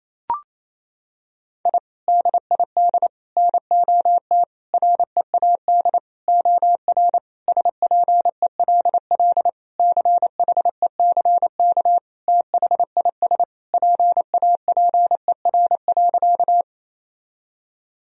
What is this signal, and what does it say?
Morse code, heard as a keyed tone, about 27 wpm, 700 Hz. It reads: I DID NOT READ OR SPELL CHECK THIS PAPER.
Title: Sentences from Top 1000 Words - Review Part 3 - 28wpm